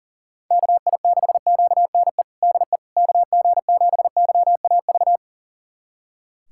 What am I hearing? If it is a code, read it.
Morse code, heard as a keyed tone, about 40 wpm, 700 Hz. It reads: KI6QNE DE KG7YAV
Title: USA Callsign DE Callsign 40wpm